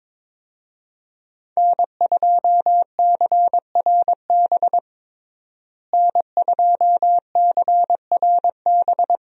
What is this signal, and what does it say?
Morse code, heard as a keyed tone, about 22 wpm, 700 Hz. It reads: N2CRB N2CRB
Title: USA Call Signs - 5 Characters - Repeated 2x - 22wpm